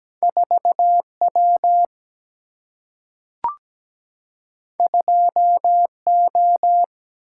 Morse code, heard as a keyed tone, about 17 wpm, 700 Hz. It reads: 4W 2O
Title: Single-Letter Single-Number 17wpm